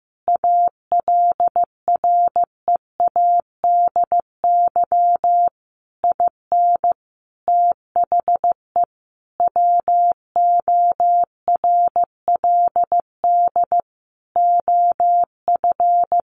Morse code, heard as a keyed tone, about 15 wpm, 700 Hz. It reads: ALREADY IN THE WORLD OF
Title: Sentences from Top 1500 Words 15wpm